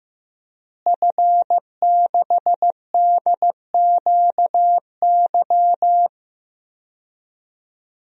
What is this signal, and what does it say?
Morse code, heard as a keyed tone, about 15 wpm, 700 Hz. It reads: F6DQY
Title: DXCC-100 Call Signs - 5 Characters - Repeated 3x - 15wpm